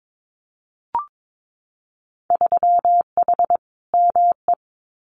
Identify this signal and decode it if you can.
Morse code, heard as a keyed tone, about 22 wpm, 700 Hz. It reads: 3H ME
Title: Winter Field Day Exchanges Encore 22wpm